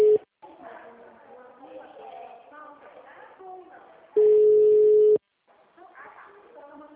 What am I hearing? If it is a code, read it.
Ringback tone (Europe, 425 Hz), with speech.